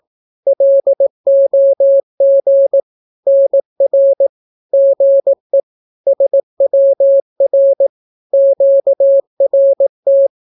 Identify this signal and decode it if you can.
Morse code, heard as a keyed tone, about 18 wpm, 550 Hz. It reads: LOG NR GE SWR QRT
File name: naukapodstaw / skroty / skroty